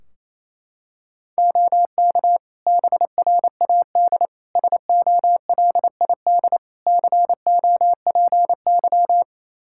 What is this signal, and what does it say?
Morse code, heard as a keyed tone, about 28 wpm, 700 Hz. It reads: OK BRAD SOLID COPY
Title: Practice QSO - Exchange 3 - More Info - 28wpm